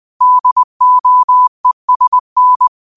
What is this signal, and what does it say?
Morse code, heard as a keyed tone, about 20 wpm, 1000 Hz. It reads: DOESN
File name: qotdfiles / 20wpm